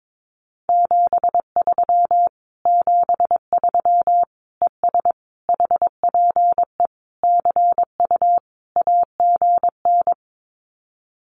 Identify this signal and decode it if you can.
Morse code, heard as a keyed tone, about 22 wpm, 700 Hz. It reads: 73 73 ES HPE CU AGN
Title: Common QSO Phrases Encore 22wpm